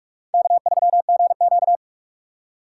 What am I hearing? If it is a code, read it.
Morse code, heard as a keyed tone, about 43 wpm, 700 Hz. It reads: K3GQ
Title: USA Call Signs - 4 Characters - 2x Character Spacing - 45wpm